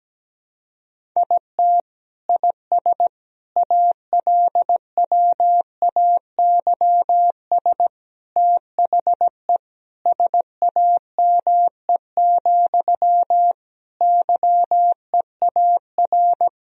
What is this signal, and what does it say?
Morse code, heard as a keyed tone, about 17 wpm, 700 Hz. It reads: IT IS ALWAYS THE SAME, YEAR
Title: Sentences from Top 1000 Words - Review Part 1 - 17wpm